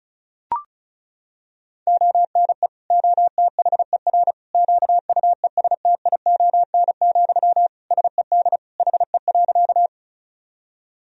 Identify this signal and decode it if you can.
Morse code, heard as a keyed tone, about 36 wpm, 700 Hz. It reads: ONE OTHER QUESTION, SED HE.
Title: Sentences from Top 400 Words and Abbreviations 35wpm